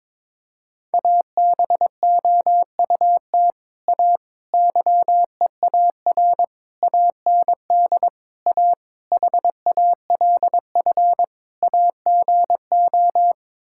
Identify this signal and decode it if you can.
Morse code, heard as a keyed tone, about 22 wpm, 700 Hz. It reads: ABOUT A YEAR AND A HALF AGO
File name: practice / 22wpm / Sets of 7 Words - 22wpm